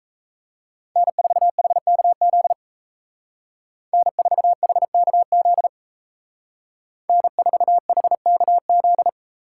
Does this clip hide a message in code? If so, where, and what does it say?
Yes: Morse code from 1.0 to 9.1 s, heard as a keyed tone, about 36 wpm, 700 Hz. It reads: N4HKZ N4HKZ N4HKZ